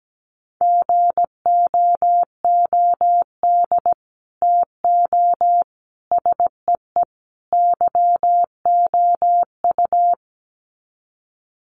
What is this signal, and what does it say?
Morse code, heard as a keyed tone, about 17 wpm, 700 Hz. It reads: GOOD TO SEE YOU
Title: Sets of 4 Words 17wpm